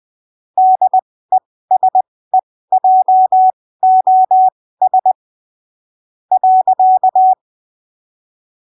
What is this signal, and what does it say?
Morse code, heard as a keyed tone, about 20 wpm, spaced as for 15 wpm, 750 Hz. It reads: DESEJOS .
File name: arquivos / texto / texto 23 20ppm